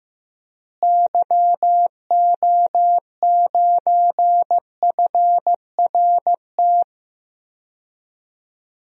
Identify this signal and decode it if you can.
Morse code, heard as a keyed tone, about 15 wpm, 700 Hz. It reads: YO9FRT
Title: DXCC-100 Call Signs - 6 Characters - Repeated 3x - 15wpm